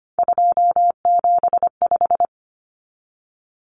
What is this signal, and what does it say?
Morse code, heard as a keyed tone, about 25 wpm, 700 Hz. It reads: 275
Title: Numbers - 3-Digit 25wpm